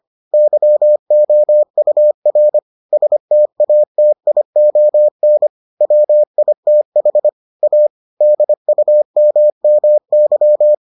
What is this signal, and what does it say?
Morse code, heard as a keyed tone, about 25 wpm, 600 Hz. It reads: YOUR STATION WITH A DUMMY